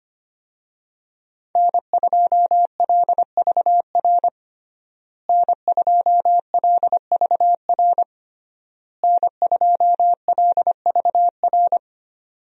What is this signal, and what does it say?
Morse code, heard as a keyed tone, about 25 wpm, 700 Hz. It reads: N2LVR N2LVR N2LVR
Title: USA Call Signs - 5 Characters - Repeated 3x - 25wpm